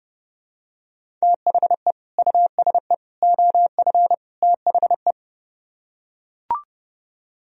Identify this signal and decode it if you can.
Morse code, heard as a keyed tone, about 30 wpm, 700 Hz. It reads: THE USE OF THE